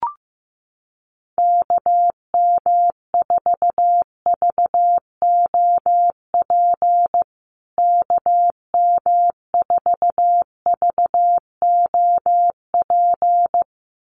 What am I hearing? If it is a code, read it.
Morse code, heard as a keyed tone, about 15 wpm, 700 Hz. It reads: KM4VOP KM4VOP
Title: USA Call Signs Repeated 2x 15wpm